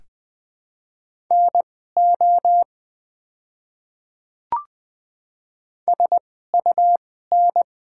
Morse code, heard as a keyed tone, about 20 wpm, spaced as for 14 wpm, 700 Hz. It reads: NO SUN
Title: 2x Character Spacing - Top 300 Words - 20wpm